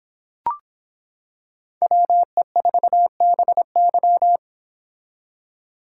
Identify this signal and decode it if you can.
Morse code, heard as a keyed tone, about 26 wpm, 700 Hz. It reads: WE4BY